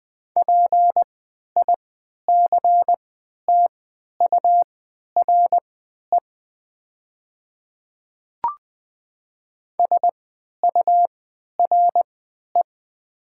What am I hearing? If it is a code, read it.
Morse code, heard as a keyed tone, about 20 wpm, spaced as for 11 wpm, 700 Hz. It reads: PICTURE SURE
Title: 3x Character Spacing - Top 300 Words - 20wpm